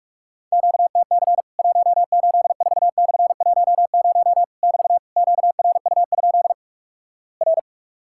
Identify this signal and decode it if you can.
Morse code, heard as a keyed tone, about 46 wpm, 700 Hz. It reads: QTC 184/10 <BT> QRU? R
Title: Worked All Europe DX Contest - QTC Exchange - 45wpm